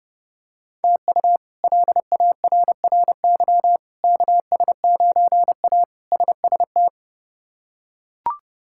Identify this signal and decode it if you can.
Morse code, heard as a keyed tone, about 30 wpm, 700 Hz. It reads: TU LARRY KS9A SST
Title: K1USN SST QSO Practice - 30wpm